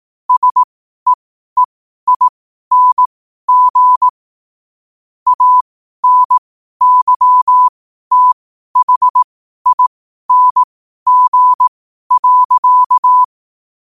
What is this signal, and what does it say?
Morse code, heard as a keyed tone, about 18 wpm, spaced as for 12 wpm, 1000 Hz. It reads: SEEING ANYTHING.
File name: qotdfiles / 12wpm